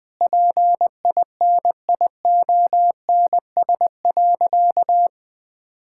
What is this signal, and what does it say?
Morse code, heard as a keyed tone, about 20 wpm, 700 Hz. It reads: PINIONS.
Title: Pangram Challenge v2 - 20wpm